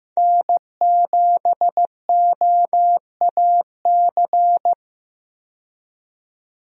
Morse code, heard as a keyed tone, about 15 wpm, 700 Hz. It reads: N7OAC
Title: USA Call Signs - 5 Characters - 15wpm